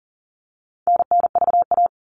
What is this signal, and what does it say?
Morse code, heard as a keyed tone, about 40 wpm, 700 Hz. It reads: NNVA
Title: All Cut Numbers - 4-Digit 40wpm